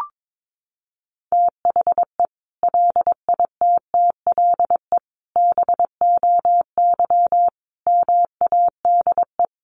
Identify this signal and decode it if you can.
Morse code, heard as a keyed tone, about 22 wpm, 700 Hz. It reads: THE LITTLE BOY MADE
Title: Sentences from Top 1400 Words 22wpm